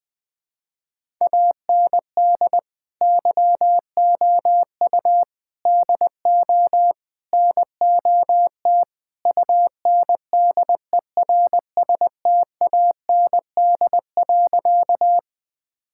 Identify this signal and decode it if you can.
Morse code, heard as a keyed tone, about 20 wpm, 700 Hz. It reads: AND YOU DO NOT UNDERSTAND.
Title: Sentences with 4-Word Phrases - 20wpm